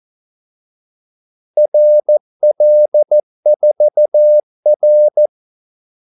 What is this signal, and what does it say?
Morse code, heard as a keyed tone, about 14 wpm, 600 Hz. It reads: RL4R
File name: Callsigns-14wpm